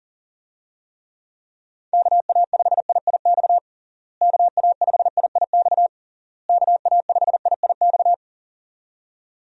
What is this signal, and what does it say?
Morse code, heard as a keyed tone, about 37 wpm, 700 Hz. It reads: KA5IIX KA5IIX KA5IIX
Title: USA Call Signs - 6 Characters - Repeated 3x - 40wpm